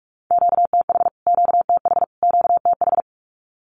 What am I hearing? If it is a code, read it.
Morse code, heard as a keyed tone, about 44 wpm, 700 Hz. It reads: QTH QTH QTH
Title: Q-Codes Repeated 3x 45wpm